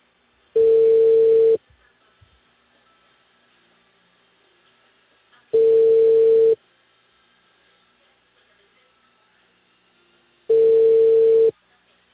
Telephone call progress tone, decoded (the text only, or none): ringback tone